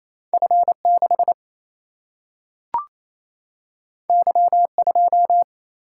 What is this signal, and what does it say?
Morse code, heard as a keyed tone, about 28 wpm, 700 Hz. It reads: F6 Y2
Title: Single-Letter Single-Number 28wpm